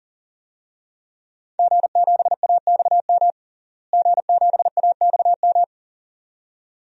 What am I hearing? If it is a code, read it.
Morse code, heard as a keyed tone, about 37 wpm, 700 Hz. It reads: G7AXM G7AXM
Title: DXCC-100 Call Signs - 5 Characters - Repeated 2x - 40wpm